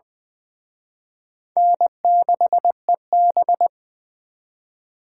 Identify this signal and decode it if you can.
Morse code, heard as a keyed tone, about 24 wpm, 700 Hz. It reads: N6EB
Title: Speed Racing - USA 1x2 Call Signs 20wpm